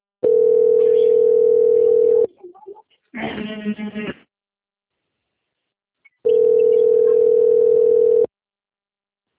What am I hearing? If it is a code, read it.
Ringback tone (North America), 440 + 480 Hz.
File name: out- - - - - .35559